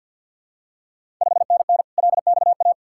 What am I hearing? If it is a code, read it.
Morse code, heard as a keyed tone, about 45 wpm, 690 Hz. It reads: 5NN LXA
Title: Ontario QSO Party Exchange 50wpm